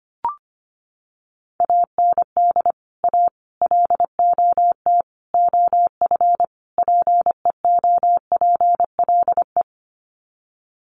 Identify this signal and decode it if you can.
Morse code, heard as a keyed tone, about 25 wpm, 700 Hz. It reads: AND A LOT OF PEOPLE
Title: Sets of 5 Words 25wpm